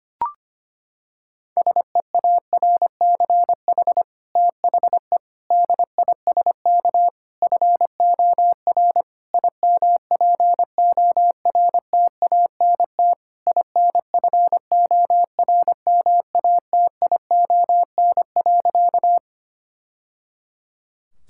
Morse code, heard as a keyed tone, about 25 wpm, 700 Hz. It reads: SEARCH THE DISK FOR IMPORTANT INFORMATION.
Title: Sentences from Top 1000 Words - Review Part 6 - 25wpm